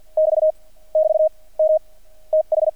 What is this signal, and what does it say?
Morse code, heard as a keyed tone, about 48 wpm, spaced as for 41 wpm, 620 Hz. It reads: <BT> <BT> M TH
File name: archived / CRSnet